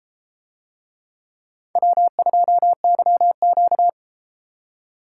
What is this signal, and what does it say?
Morse code, heard as a keyed tone, about 32 wpm, 700 Hz. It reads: W2YQ